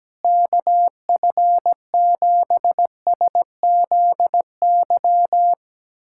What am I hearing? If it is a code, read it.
Morse code, heard as a keyed tone, about 17 wpm, 700 Hz. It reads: KF7SZY
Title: USA Callsign DE Callsign - 6 Characters - 17wpm